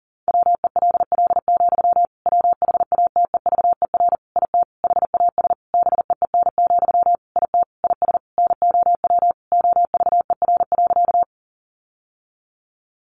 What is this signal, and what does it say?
Morse code, heard as a keyed tone, about 40 wpm, 700 Hz. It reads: WELL, WHATEVER IT HAS BEEN, IT IS NOW OVER.
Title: Sentences from Top 1100 Words 40wpm